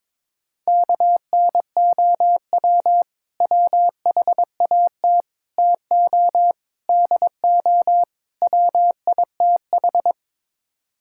Morse code, heard as a keyed tone, about 22 wpm, 700 Hz. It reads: KNOW WHAT TO DO WITH
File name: practice / 22wpm / Build A Phrase - 5-Word Phrases - 22wpm